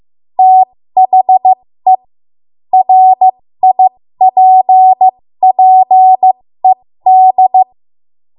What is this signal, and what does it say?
Morse code, heard as a keyed tone, about 15 wpm, spaced as for 13 wpm, 750 Hz. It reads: THE RIPPED
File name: Morse / 13WPM